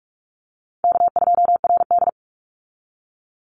Morse code, heard as a keyed tone, about 44 wpm, 700 Hz. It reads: K2RD
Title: Call Signs Repeated 3x - 45wpm